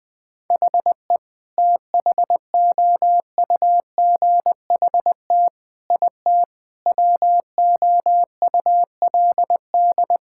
Morse code, heard as a keyed tone, about 20 wpm, 700 Hz. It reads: HE THOUGHT IT WOULD
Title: Sentences with 8-Word Phrases - 20wpm